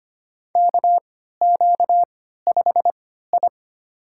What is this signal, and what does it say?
Morse code, heard as a keyed tone, about 25 wpm, spaced as for 14 wpm, 700 Hz. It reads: KQ5I
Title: USA Call Signs - 4 Characters - 3x Character Spacing - 25wpm